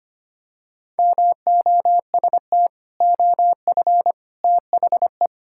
Morse code, heard as a keyed tone, about 25 wpm, 700 Hz. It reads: MOST OF THE
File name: practice / 25wpm / Sentences with 3-Word Phrases - 25wpm